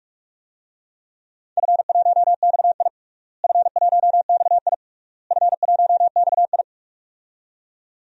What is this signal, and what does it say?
Morse code, heard as a keyed tone, about 46 wpm, 700 Hz. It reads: F1XI F1XI F1XI
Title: DXCC-100 Call Signs - 4 Characters - Repeated 3x - 45wpm